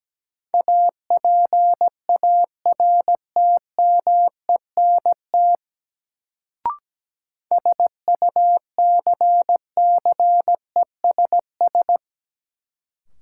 Morse code, heard as a keyed tone, about 17 wpm, 700 Hz. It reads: APARTMENT SUCCESS
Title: Sentences from Top 900 Words - New Word Drill - 17wpm